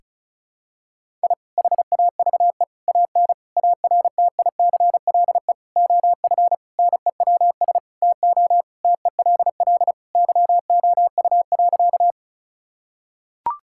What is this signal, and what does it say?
Morse code, heard as a keyed tone, about 36 wpm, 700 Hz. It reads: I HAVE AN ARTICLE OF NEWS TO TELL YOU.
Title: Sentences from Top 1000 Words - Review Part 5 - 35wpm